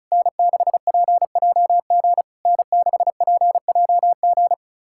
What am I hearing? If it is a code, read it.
Morse code, heard as a keyed tone, about 36 wpm, 700 Hz. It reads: N6PJG N6PJG
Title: SOTA QSO Practice 35wpm